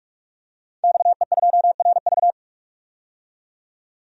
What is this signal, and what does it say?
Morse code, heard as a keyed tone, about 42 wpm, 700 Hz. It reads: XE2RV
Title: DXCC-100 Call Signs - 5 Characters - 45wpm